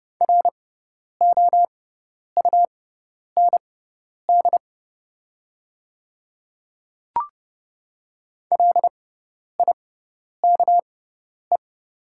Morse code, heard as a keyed tone, about 30 wpm, spaced as for 10 wpm, 700 Hz. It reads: ROUND LIKE
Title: 6x Character Spacing - Top 300 Words - 30wpm